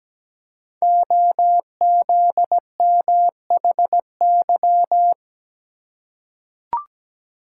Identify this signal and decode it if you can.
Morse code, heard as a keyed tone, about 17 wpm, 700 Hz. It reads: OZMHY
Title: Code Groups - 5 Letters - 17wpm